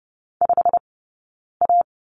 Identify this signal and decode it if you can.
Morse code, heard as a keyed tone, about 30 wpm, spaced as for 9 wpm, 690 Hz. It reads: 5A
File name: course / 30:10wpm / 086 Lesson 15d - Number 5 - Two Characters 30wpm